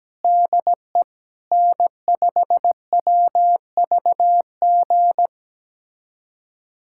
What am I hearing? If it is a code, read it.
Morse code, heard as a keyed tone, about 17 wpm, 700 Hz. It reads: DE N5WVG
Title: USA Callsign DE Callsign - 5 Characters - 17wpm